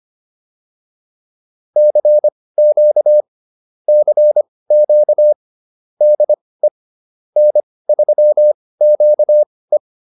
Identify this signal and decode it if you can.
Morse code, heard as a keyed tone, about 25 wpm, spaced as for 18 wpm, 600 Hz. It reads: CQ CQ DE N3QE